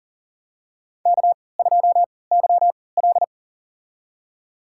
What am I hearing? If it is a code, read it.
Morse code, heard as a keyed tone, about 40 wpm, spaced as for 23 wpm, 700 Hz. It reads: K2YL
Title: USA Call Signs - 3x Character Spacing - 40wpm